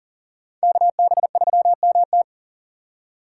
Morse code, heard as a keyed tone, about 40 wpm, 700 Hz. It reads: KB3MT
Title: USA Call Signs - 3x Character Spacing - 40wpm